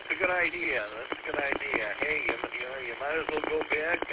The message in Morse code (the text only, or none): none